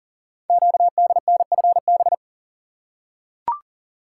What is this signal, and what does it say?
Morse code, heard as a keyed tone, about 38 wpm, 700 Hz. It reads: QDNFB
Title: Code Groups - 5 Letters - 40wpm